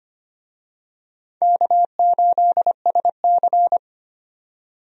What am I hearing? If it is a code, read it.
Morse code, heard as a keyed tone, about 25 wpm, 700 Hz. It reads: K8SC
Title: USA Call Signs - 4 Characters - 25wpm